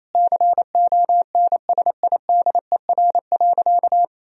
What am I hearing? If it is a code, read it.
Morse code, heard as a keyed tone, about 28 wpm, 700 Hz. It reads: CONSIDER.
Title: Sentences from Top 1000 Words - Review Part 4 - 28wpm